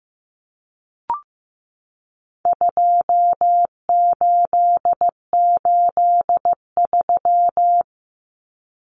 Morse code, heard as a keyed tone, about 15 wpm, 700 Hz. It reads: 2883
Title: Common Cut Numbers - 4-Digit 15wpm